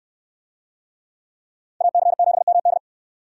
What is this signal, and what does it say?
Morse code, heard as a keyed tone, about 68 wpm, 700 Hz. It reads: AC7MD